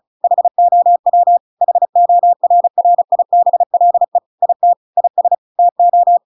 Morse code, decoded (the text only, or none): HOW HORRIBLE IT IS TO